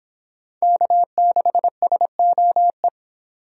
Morse code, heard as a keyed tone, about 24 wpm, 700 Hz. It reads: K6SOE